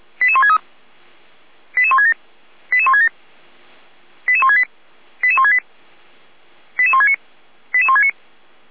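Five-tone selective call (ZVEI), heard as quick runs of five tones, tones of about 70 ms. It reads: 88163 88167 88167 88168 88168 88169 88169
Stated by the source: ZVEI five-tone selective call received on SDR.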